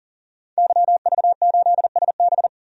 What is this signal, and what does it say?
Morse code, heard as a keyed tone, about 37 wpm, 700 Hz. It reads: YV8SB
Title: DXCC-100 Call Signs - 5 Characters - 40wpm